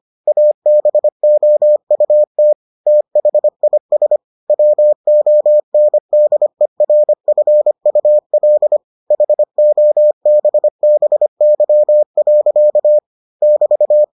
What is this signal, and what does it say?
Morse code, heard as a keyed tone, about 25 wpm, 600 Hz. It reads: ABOUT THIS WONDERFUL HOBBY. <BT>